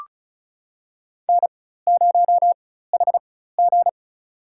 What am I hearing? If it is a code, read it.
Morse code, heard as a keyed tone, about 36 wpm, spaced as for 16 wpm, 700 Hz. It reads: N0HG
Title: USA Call Signs - 4x Character Spacing - 35wpm